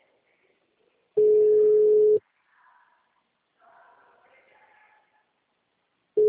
Ringback tone (Europe, 425 Hz).